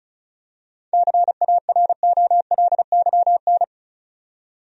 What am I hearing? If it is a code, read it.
Morse code, heard as a keyed tone, about 36 wpm, 700 Hz. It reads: CAROLYN
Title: USA Common Names - 35wpm